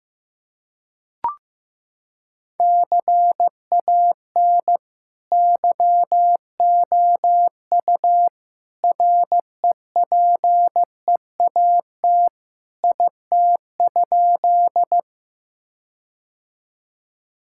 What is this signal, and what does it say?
Morse code, heard as a keyed tone, about 15 wpm, 700 Hz. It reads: CAN YOU REPEAT IT?
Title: Sentences from Top 1600 Words 15wpm